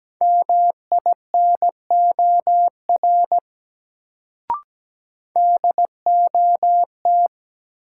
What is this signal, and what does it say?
Morse code, heard as a keyed tone, about 17 wpm, 700 Hz. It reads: MINOR DOT
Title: Sentences from Top 1500 Words - New Word Drill Alt - 17wpm